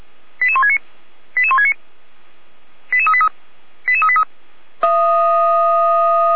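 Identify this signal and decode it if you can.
Five-tone selective call (ZVEI), heard as quick runs of five tones, tones of about 70 ms. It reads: 88169 88169 88383 88383
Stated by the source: ZVEI five-tone selective call received by SDR.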